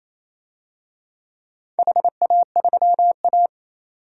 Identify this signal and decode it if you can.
Morse code, heard as a keyed tone, about 28 wpm, 700 Hz. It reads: HA3A
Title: DXCC-100 Call Signs - 4 Characters - 28wpm